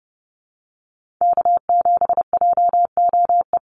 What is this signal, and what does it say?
Morse code, heard as a keyed tone, about 30 wpm, 700 Hz. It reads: K7JOE
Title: Call Signs Repeated 2x - 30wpm